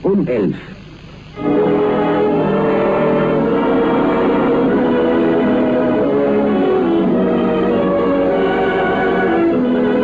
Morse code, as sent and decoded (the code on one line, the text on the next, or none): none